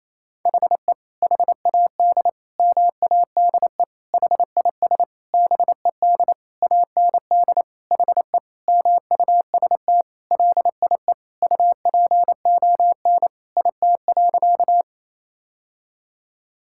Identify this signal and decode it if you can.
Morse code, heard as a keyed tone, about 27 wpm, 700 Hz. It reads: HE HAD MADE HIS BED AND HE MUST LIE UPON IT.
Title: Sentences from Top 1600 Words 28wpm